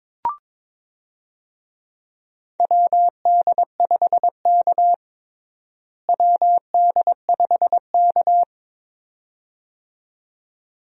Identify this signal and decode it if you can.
Morse code, heard as a keyed tone, about 21 wpm, 700 Hz. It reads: WD5K WD5K
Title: USA Call Signs - 4 Characters - Repeated 2x - 22wpm